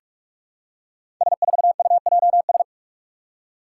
Morse code, heard as a keyed tone, about 42 wpm, 700 Hz. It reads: S4UJS
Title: Five Challenging Letter-Number Combinations 45wpm